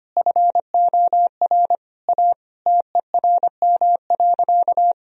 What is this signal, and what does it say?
Morse code, heard as a keyed tone, about 25 wpm, 700 Hz. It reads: FOR A TERM.
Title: Sentences from Top 1000 Words - Review Part 4 - 25wpm